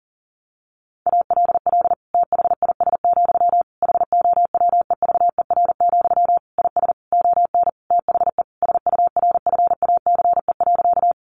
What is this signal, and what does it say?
Morse code, heard as a keyed tone, about 37 wpm, 700 Hz. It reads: ALL THIS, HOWEVER, IS ON THE SURFACE.
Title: Sentences from Top 600 Words 40wpm